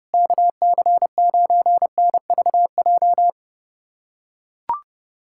Morse code, heard as a keyed tone, about 32 wpm, 700 Hz. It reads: KC9NVJ
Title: USA Callsign DE Callsign - 6 Characters - 30wpm